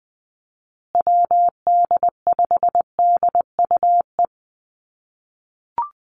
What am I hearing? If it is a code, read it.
Morse code, heard as a keyed tone, about 20 wpm, 700 Hz. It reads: WD5DUE
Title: USA Call Signs Repeated 3x 20wpm